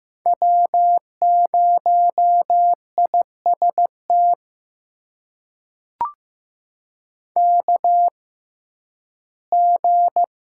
Morse code, heard as a keyed tone, about 15 wpm, spaced as for 11 wpm, 700 Hz. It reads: W0IST KG
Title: USA Call Signs - 5 Characters - 6x Character Spacing - 15wpm